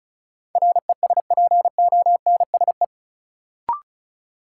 Morse code, heard as a keyed tone, about 36 wpm, 700 Hz. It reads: RESPONSE
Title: Sentences from Top 1400 Words - New Word Drill Alt - 35wpm